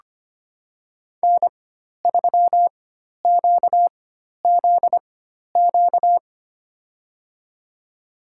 Morse code, heard as a keyed tone, about 25 wpm, spaced as for 12 wpm, 700 Hz. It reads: N3QZQ
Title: USA Call Signs - 4x Character Spacing - 25wpm